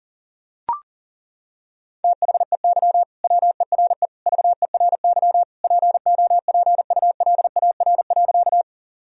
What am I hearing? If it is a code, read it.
Morse code, heard as a keyed tone, about 40 wpm, 700 Hz. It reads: THEY WERE VERY POPULAR.
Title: Sentences from Top 1000 Words - Review Part 3 - 40wpm